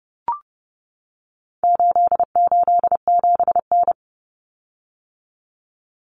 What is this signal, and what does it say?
Morse code, heard as a keyed tone, about 30 wpm, 700 Hz. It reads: 887N
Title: Common Cut Numbers - 4-Digit 30wpm